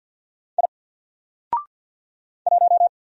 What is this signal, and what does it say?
Morse code, heard as a keyed tone, about 50 wpm, 700 Hz. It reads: I 1
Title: Single Letter-Number 50wpm